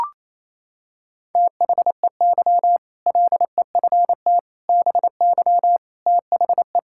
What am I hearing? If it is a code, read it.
Morse code, heard as a keyed tone, about 28 wpm, 700 Hz. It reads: THEY LEFT BY THE
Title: Sentences from Top 1600 Words 28wpm